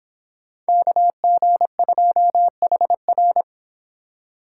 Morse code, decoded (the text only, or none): KG2HR